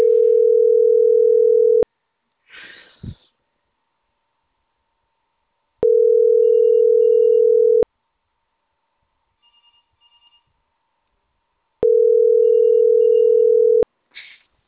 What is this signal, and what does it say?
Ringback tone (North America), 440 + 480 Hz.